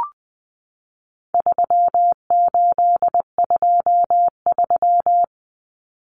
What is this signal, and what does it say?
Morse code, heard as a keyed tone, about 20 wpm, 700 Hz. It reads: 3823
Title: Common Cut Numbers - 4-Digit 20wpm